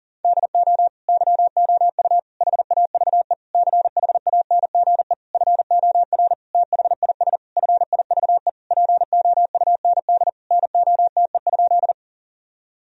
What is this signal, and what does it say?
Morse code, heard as a keyed tone, about 40 wpm, 700 Hz. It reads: DO YOU HAVE CHANGE FOR THIS FIVE POUND NOTE?
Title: Sentences from Top 1000 Words - Review Part 2 - 40wpm